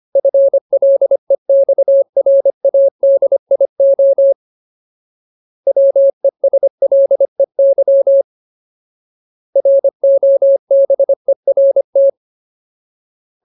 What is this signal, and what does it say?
Morse code, heard as a keyed tone, about 25 wpm, 550 Hz. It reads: FLEXRADIO WESLEY ROBERT
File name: Ultimate Word List / 25WPM Ultimate